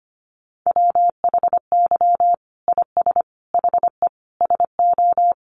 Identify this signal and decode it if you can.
Morse code, heard as a keyed tone, about 25 wpm, 700 Hz. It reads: WHY IS HE SO
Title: Sentences from Top 1200 Words 25wpm